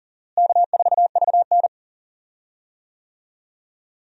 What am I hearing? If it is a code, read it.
Morse code, heard as a keyed tone, about 37 wpm, 700 Hz. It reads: K4VN
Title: USA Call Signs - 4 Characters - Repeated 2x - 40wpm